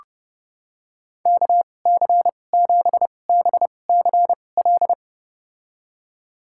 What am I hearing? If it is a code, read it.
Morse code, heard as a keyed tone, about 30 wpm, spaced as for 22 wpm, 700 Hz. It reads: KC7BCL
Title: USA Call Signs - 6 Characters - 2x Character Spacing - 30wpm